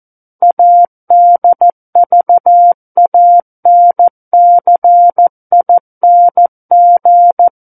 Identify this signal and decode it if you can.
Morse code, heard as a keyed tone, about 14 wpm, 700 Hz. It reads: ADVANCING
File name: Book One / 05-Chapter 5 - 14 WPM